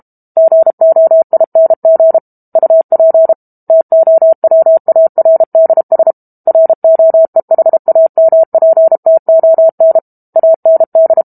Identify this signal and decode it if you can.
Morse code, heard as a keyed tone, about 32 wpm, 650 Hz. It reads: GOING UP TOWARDS ROEHAMPTON AND
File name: Book Two / 06-Chapter 6 - 32 WPM